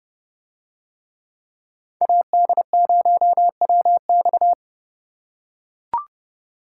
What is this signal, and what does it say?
Morse code, heard as a keyed tone, about 30 wpm, 700 Hz. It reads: AD0WX